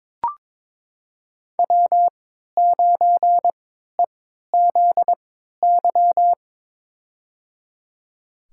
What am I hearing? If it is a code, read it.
Morse code, heard as a keyed tone, about 22 wpm, spaced as for 13 wpm, 700 Hz. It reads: W9EZY
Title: USA Call Signs - 5 Characters - 3x Character Spacing - 22wpm